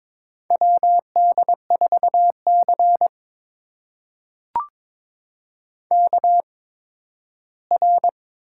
Morse code, heard as a keyed tone, about 21 wpm, spaced as for 13 wpm, 700 Hz. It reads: WD4C KR
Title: USA Call Signs - 4 Characters - 8x Character Spacing - 22wpm